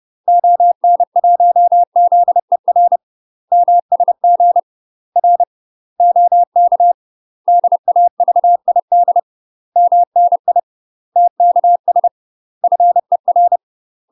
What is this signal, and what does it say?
Morse code, heard as a keyed tone, about 30 wpm, 700 Hz. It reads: ON1ZER MSG R OK DAVID MNI TKS FER
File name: UK MASTER QSOs TRAINING / 30WPM / UK MASTER QSOs 0003